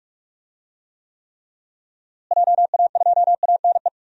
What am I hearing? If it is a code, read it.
Morse code, heard as a keyed tone, about 47 wpm, 700 Hz. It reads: JA2ANE
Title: DXCC-100 Call Signs - 6 Characters - Repeated 3x - 45wpm